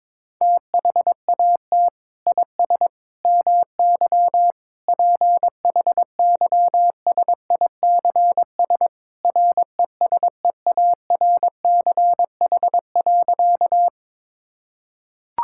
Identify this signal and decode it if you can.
Morse code, heard as a keyed tone, about 22 wpm, 700 Hz. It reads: THAT IS MY PHYSICS RESEARCH.
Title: Sentences from Top 1000 Words - Review Part 5 - 22wpm